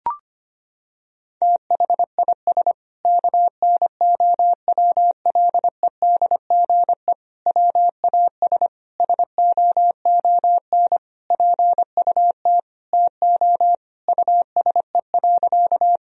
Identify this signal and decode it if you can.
Morse code, heard as a keyed tone, about 25 wpm, 700 Hz. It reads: THIS KNOWLEDGE WAS SOON PUT TO USE.
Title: Sentences from Top 1000 Words - Review Part 3 - 25wpm